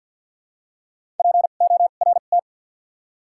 Morse code, heard as a keyed tone, about 48 wpm, 700 Hz. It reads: P O R T
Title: 2x Character Spacing - Top 500 Words - 50wpm